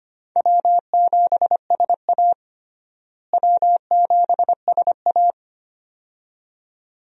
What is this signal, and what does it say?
Morse code, heard as a keyed tone, about 25 wpm, 700 Hz. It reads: W7SA W7SA
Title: USA Call Signs - 4 Characters - Repeated 2x - 25wpm